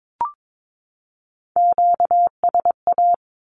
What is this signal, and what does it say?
Morse code, heard as a keyed tone, about 22 wpm, 700 Hz. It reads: QSA
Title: Q-Codes 22wpm